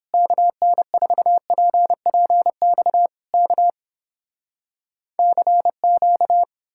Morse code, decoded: KN4PPX K CQ

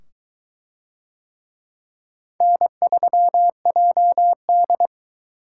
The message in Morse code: N3JD

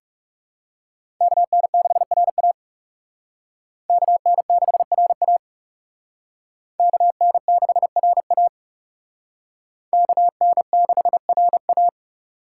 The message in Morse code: KN6RA KN6RA KN6RA KN6RA

